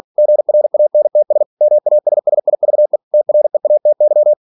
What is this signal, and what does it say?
Morse code, heard as a keyed tone, about 46 wpm, 600 Hz. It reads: GRANTS MASSIVE TREATY
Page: Never 47wpm